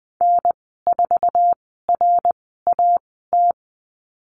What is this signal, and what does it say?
Morse code, heard as a keyed tone, about 20 wpm, spaced as for 15 wpm, 700 Hz. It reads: N4RAT
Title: 062 Lesson 11g - Letter h - Call Signs 20wpm